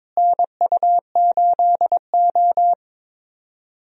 Morse code, heard as a keyed tone, about 22 wpm, 700 Hz. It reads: NU8O
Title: USA Call Signs - 4 Characters - Repeated 3x - 22wpm